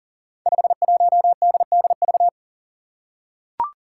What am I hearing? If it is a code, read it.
Morse code, heard as a keyed tone, about 40 wpm, 700 Hz. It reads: 51DDV
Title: Five Challenging Letter-Number Combinations 40wpm